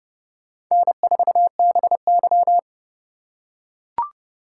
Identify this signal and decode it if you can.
Morse code, heard as a keyed tone, about 28 wpm, 700 Hz. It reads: N4BY